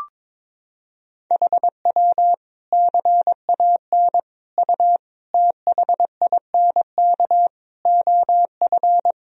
Morse code, heard as a keyed tone, about 22 wpm, 700 Hz. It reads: HW CAN U THINK OF
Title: Sentences from Top 200 Words and Abbreviations 22wpm